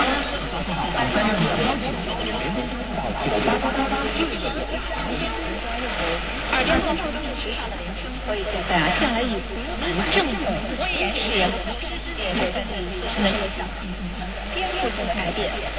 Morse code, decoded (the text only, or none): none